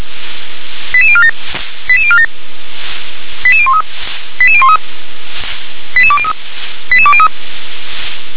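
Five-tone selective call (ZVEI), heard as quick runs of five tones, tones of about 70 ms. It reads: 88047 88047 88013 88013 88203 88203
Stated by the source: ZVEI five-tone call 88037 received on SDR.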